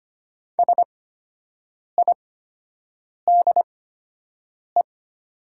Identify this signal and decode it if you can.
Morse code, heard as a keyed tone, about 25 wpm, spaced as for 7 wpm, 700 Hz. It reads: SIDE
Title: 8x Character Spacing - Top 200 Words - 25wpm